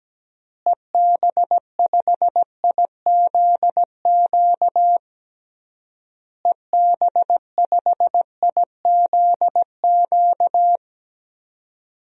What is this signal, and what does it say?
Morse code, heard as a keyed tone, about 17 wpm, 700 Hz. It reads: EB5IZQ EB5IZQ
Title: DXCC-100 Call Signs - 6 Characters - Repeated 3x - 17wpm